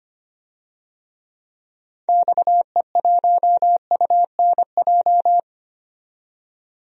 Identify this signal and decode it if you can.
Morse code, heard as a keyed tone, about 25 wpm, 700 Hz. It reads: XE1UNJ
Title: DXCC-100 Call Signs - 6 Characters - 25wpm